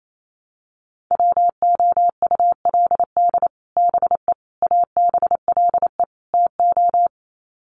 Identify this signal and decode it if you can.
Morse code, heard as a keyed tone, about 28 wpm, 700 Hz. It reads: WOULD BE ABLE TO
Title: Sets of 4 Words 28wpm